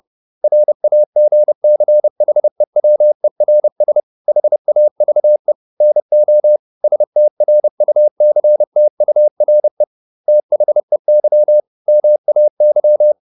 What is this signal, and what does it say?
Morse code, heard as a keyed tone, about 30 wpm, 600 Hz. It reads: RAGCHEWERS HAVE NO STRUCTURE THEY MAY